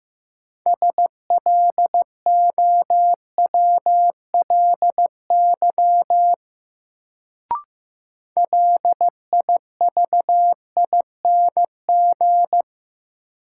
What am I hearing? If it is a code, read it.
Morse code, heard as a keyed tone, about 15 wpm, 700 Hz. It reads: SLOWLY LIVING
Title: Sentences from Top 1100 Words - New Word Drill Alt - 15wpm